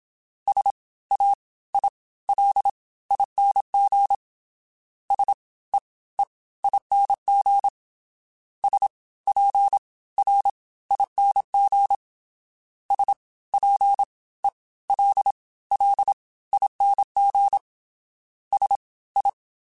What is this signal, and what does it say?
Morse code, heard as a keyed tone, about 27 wpm, spaced as for 18 wpm, 800 Hz. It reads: SAILING SEEING SPRING SPELLING SI